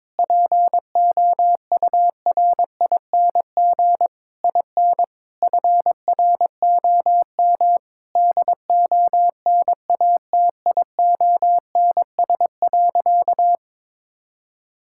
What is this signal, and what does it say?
Morse code, heard as a keyed tone, about 22 wpm, 700 Hz. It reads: POURING IN FROM DONATIONS.
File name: Sentences with 9-Word Phrases - 22wpm